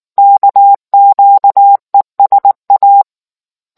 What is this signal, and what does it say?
Morse code, heard as a keyed tone, about 19 wpm, 800 Hz. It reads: KQESA